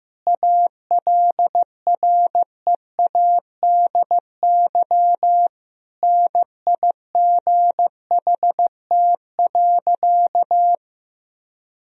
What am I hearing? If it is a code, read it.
Morse code, heard as a keyed tone, about 15 wpm, 700 Hz. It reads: ALREADY NIGHT.
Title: Sentences from Top 1000 Words - Review Part 1 - 15wpm